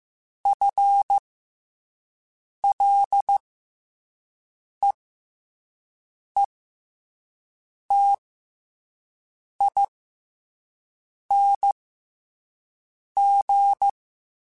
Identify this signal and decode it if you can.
Morse code, heard as a keyed tone, about 15 wpm, spaced as for 5 wpm, 800 Hz. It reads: FLEETING
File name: FL / FL mp3s / FL Prefixes 3x2 Twice @5